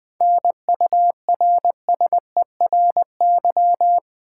Morse code, heard as a keyed tone, about 20 wpm, 700 Hz. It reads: NURSERY
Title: 7 Letter Words - First 3 Letters Given - 20wpm